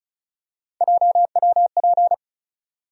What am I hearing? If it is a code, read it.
Morse code, heard as a keyed tone, about 36 wpm, 700 Hz. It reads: JWP
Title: USA Call Sign Postfixes - 3 Letters 35wpm